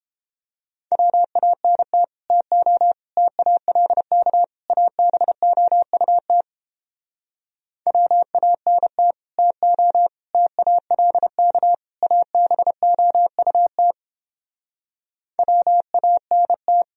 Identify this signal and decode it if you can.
Morse code, heard as a keyed tone, about 29 wpm, 700 Hz. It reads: WANT TO TALK ABOUT WANT TO TALK ABOUT WANT